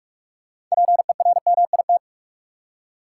Morse code, heard as a keyed tone, about 46 wpm, 700 Hz. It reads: PERMIT
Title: Six Letter Words - 45wpm